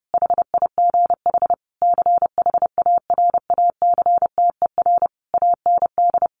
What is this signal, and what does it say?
Morse code, heard as a keyed tone, about 30 wpm, 700 Hz. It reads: HIGH CHARACTER AND
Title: Sentences from Top 1200 Words 30wpm